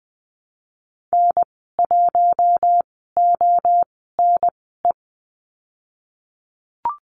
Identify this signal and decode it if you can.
Morse code, heard as a keyed tone, about 20 wpm, spaced as for 15 wpm, 700 Hz. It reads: N1ONE
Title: 048 Lesson 9g - Number 4 - Call Signs 20wpm